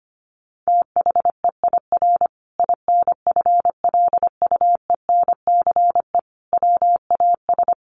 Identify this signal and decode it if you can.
Morse code, heard as a keyed tone, about 25 wpm, 700 Hz. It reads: THEIR INFLUENCE WAS
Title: Sentences from Top 1400 Words 25wpm